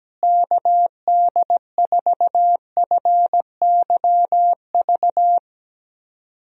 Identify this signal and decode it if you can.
Morse code, heard as a keyed tone, about 17 wpm, 700 Hz. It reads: KD4FYV
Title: USA Call Signs - 6 Characters - Repeated 3x - 17wpm